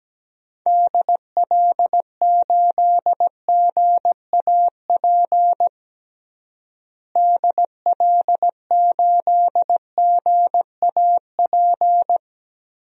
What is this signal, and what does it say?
Morse code, heard as a keyed tone, about 17 wpm, 700 Hz. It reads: DL8GAP DL8GAP
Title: DXCC-100 Call Signs - 6 Characters - Repeated 2x - 17wpm